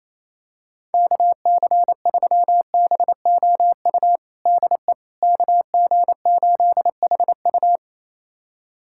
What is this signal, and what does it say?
Morse code, heard as a keyed tone, about 27 wpm, 700 Hz. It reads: KC3BOU DE KG8HU
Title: USA Callsign DE Callsign 28wpm